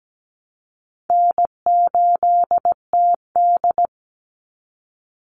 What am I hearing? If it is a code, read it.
Morse code, heard as a keyed tone, about 17 wpm, 700 Hz. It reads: N8TD
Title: Easy Call Signs Repeated 3x - 17wpm